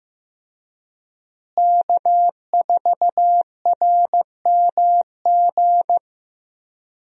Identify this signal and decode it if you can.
Morse code, heard as a keyed tone, about 15 wpm, 700 Hz. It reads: K4RMG
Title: USA Call Signs - 5 Characters - Repeated 3x - 15wpm